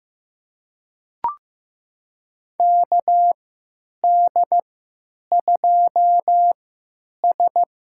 Morse code, heard as a keyed tone, about 15 wpm, spaced as for 9 wpm, 700 Hz. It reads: KD2S
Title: USA Call Signs - 3x Character Spacing - 15wpm